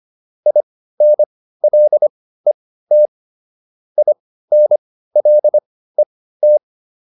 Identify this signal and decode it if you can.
Morse code, heard as a keyed tone, about 25 wpm, spaced as for 15 wpm, 600 Hz. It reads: INLET INLET
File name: IN5 15